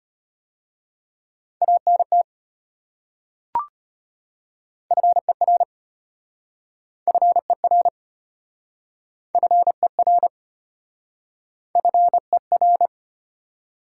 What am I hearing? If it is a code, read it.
Morse code, heard as a keyed tone, about 34 wpm, 700 Hz. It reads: ANT FER FER FER FER